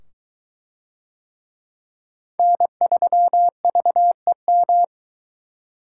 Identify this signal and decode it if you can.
Morse code, heard as a keyed tone, about 21 wpm, 700 Hz. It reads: N3VEM